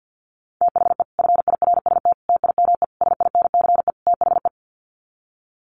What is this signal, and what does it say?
Morse code, heard as a keyed tone, about 50 wpm, 700 Hz. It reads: THE FIRST TIME SINCE THE
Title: Sets of 5 Words Encore 50wpm